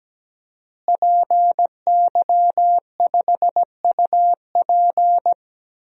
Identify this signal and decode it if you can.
Morse code, heard as a keyed tone, about 17 wpm, 700 Hz. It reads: PY5UP
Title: DXCC-100 Call Signs - 5 Characters - Repeated 3x - 17wpm